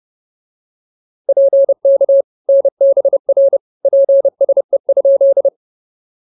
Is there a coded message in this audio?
Yes: Morse code from 1.3 to 5.5 s, heard as a keyed tone, about 29 wpm, 560 Hz. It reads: PK NBR PSE?